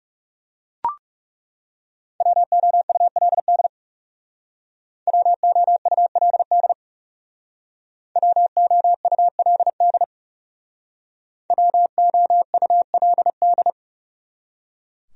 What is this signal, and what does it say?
Morse code, heard as a keyed tone, about 38 wpm, 700 Hz. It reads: WOULD WOULD WOULD WOULD